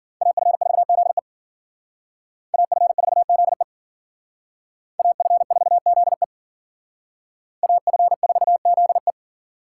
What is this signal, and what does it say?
Morse code, heard as a keyed tone, about 51 wpm, 700 Hz. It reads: AF4ZE AF4ZE AF4ZE AF4ZE